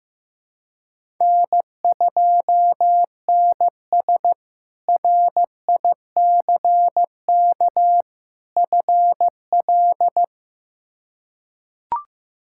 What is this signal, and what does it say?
Morse code, heard as a keyed tone, about 16 wpm, 700 Hz. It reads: N2NS RICK FL
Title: K1USN SST QSO Practice - 15wpm